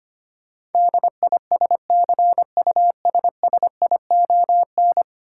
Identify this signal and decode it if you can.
Morse code, heard as a keyed tone, about 25 wpm, 700 Hz. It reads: DISCUSSION
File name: 10 Letter Words - First 3 Letters Given - 25wpm